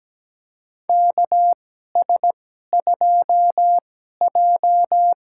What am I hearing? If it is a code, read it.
Morse code, heard as a keyed tone, about 17 wpm, spaced as for 12 wpm, 700 Hz. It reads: KS2J
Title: USA Call Signs - 4 Characters - 2x Character Spacing - 17wpm